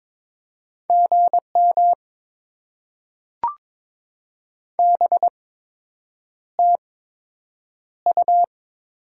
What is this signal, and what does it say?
Morse code, heard as a keyed tone, about 22 wpm, spaced as for 8 wpm, 700 Hz. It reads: GM BTU
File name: Top 20 QSO Elements - 8x Character Spacing - 22wpm